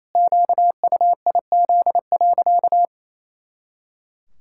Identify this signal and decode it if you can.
Morse code, heard as a keyed tone, about 28 wpm, 700 Hz. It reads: QUIZ.
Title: Pangram Challenge v2 - 28wpm